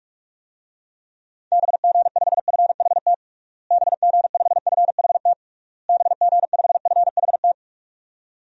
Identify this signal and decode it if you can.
Morse code, heard as a keyed tone, about 46 wpm, 700 Hz. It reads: BG5FHT BG5FHT BG5FHT
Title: DXCC-100 Call Signs - 6 Characters - Repeated 3x - 45wpm